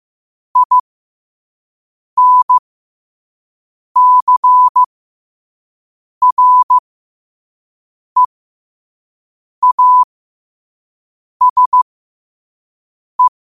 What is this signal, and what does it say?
Morse code, heard as a keyed tone, about 15 wpm, spaced as for 5 wpm, 1000 Hz. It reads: INCREASE
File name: qotdfiles / 5wpm